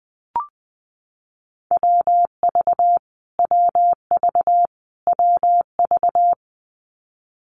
Morse code, heard as a keyed tone, about 20 wpm, 700 Hz. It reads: WV WV WV
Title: US States Repeated 3x 20wpm